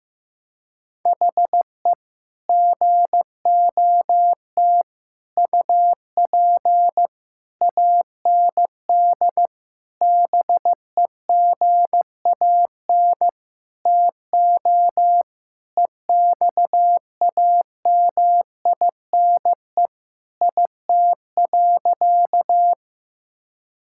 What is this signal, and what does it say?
Morse code, heard as a keyed tone, about 16 wpm, 700 Hz. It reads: HE GOT UP AND BEGAN TO EXAMINE IT.
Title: Sentences from Top 1600 Words 15wpm